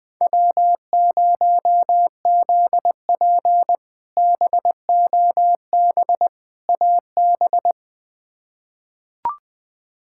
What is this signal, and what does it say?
Morse code, heard as a keyed tone, about 20 wpm, 700 Hz. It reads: W0ZP BOB AB
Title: K1USN SST QSO Practice - 20wpm